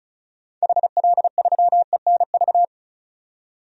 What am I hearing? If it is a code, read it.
Morse code, heard as a keyed tone, about 34 wpm, 700 Hz. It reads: HL3ENV
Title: DXCC-100 Call Signs - 6 Characters - 35wpm